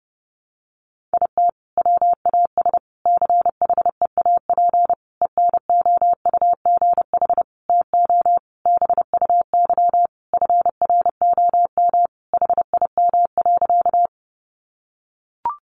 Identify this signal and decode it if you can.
Morse code, heard as a keyed tone, about 30 wpm, 700 Hz. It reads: IT WAS CHEAP ENOUGH TO BUY FROM HIM.
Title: Sentences from Top 1200 Words 30wpm